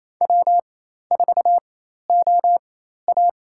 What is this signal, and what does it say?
Morse code, heard as a keyed tone, about 28 wpm, spaced as for 13 wpm, 700 Hz. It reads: W4OA
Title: USA Call Signs - 4x Character Spacing - 28wpm